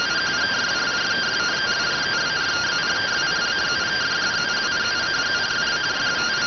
Two-tone teleprinter signal, about 170 Hz shift.